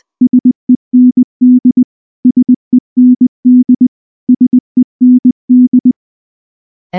Morse code, heard as a keyed tone, about 20 wpm, 250 Hz. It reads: SEND SEND SEND
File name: morse / morse-training-20WPM-2